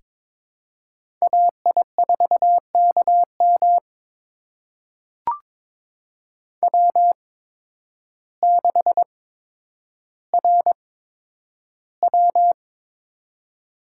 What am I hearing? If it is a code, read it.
Morse code, heard as a keyed tone, about 22 wpm, spaced as for 10 wpm, 700 Hz. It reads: AI4KM W6RW
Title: USA Call Signs - 8x Character Spacing - 22wpm